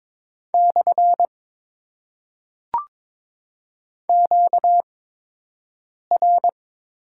Morse code, heard as a keyed tone, about 22 wpm, 700 Hz. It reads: / Q R
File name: Top 90 QSO Elements - 8x Character Spacing - 22wpm